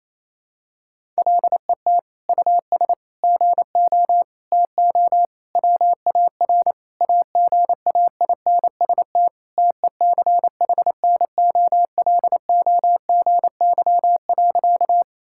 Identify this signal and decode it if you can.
Morse code, heard as a keyed tone, about 28 wpm, 700 Hz. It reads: LET US GO TO WAR AGAINST TECHNOLOGY.
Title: Sentences from Top 1000 Words - Review Part 5 - 28wpm